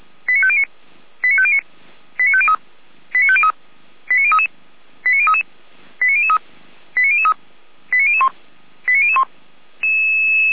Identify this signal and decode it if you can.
Five-tone selective call (ZVEI), heard as quick runs of five tones, tones of about 70 ms. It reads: 89509 89509 89503 89503 89033 89033 89003 89003 89001 89001